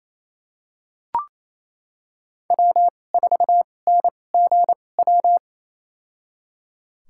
Morse code, heard as a keyed tone, about 28 wpm, spaced as for 20 wpm, 700 Hz. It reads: W4NGW
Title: USA Call Signs - 5 Characters - 2x Character Spacing - 28wpm